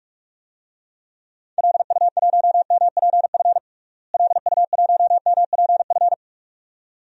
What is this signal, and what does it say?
Morse code, heard as a keyed tone, about 46 wpm, 700 Hz. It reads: LU1MPF LU1MPF
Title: DXCC-100 Call Signs - 6 Characters - Repeated 2x - 45wpm